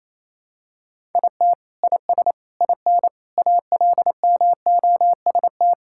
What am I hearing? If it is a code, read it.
Morse code, heard as a keyed tone, about 28 wpm, 700 Hz. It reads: IT IS IN ALMOST
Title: Sentences from Top 1000 Words - Review Part 5 - 28wpm